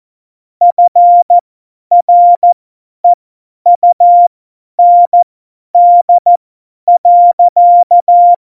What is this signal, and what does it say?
Morse code, heard as a keyed tone, about 14 wpm, spaced as for 10 wpm, 700 Hz. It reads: FREUND.
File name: Morsekurs / Witze / Witz 17.Juli 2018, 50 F70